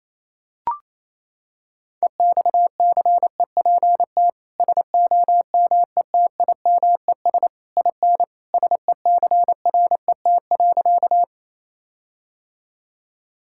Morse code, heard as a keyed tone, about 28 wpm, 700 Hz. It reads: EXCEPT SOMETIMES IN SECRET.
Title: Sentences from Top 1600 Words 28wpm